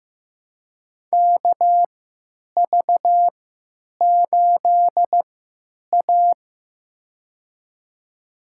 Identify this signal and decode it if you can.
Morse code, heard as a keyed tone, about 15 wpm, spaced as for 9 wpm, 700 Hz. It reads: KV8A
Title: USA Call Signs - 4 Characters - 3x Character Spacing - 15wpm